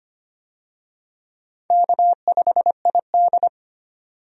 Morse code, heard as a keyed tone, about 25 wpm, 700 Hz. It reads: K5ID
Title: USA Call Signs - 4 Characters - 25wpm